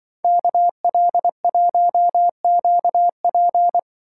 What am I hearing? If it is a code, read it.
Morse code, heard as a keyed tone, about 24 wpm, 700 Hz. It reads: KL1QP
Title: Speed Racing - USA 2x2 Call Signs 20wpm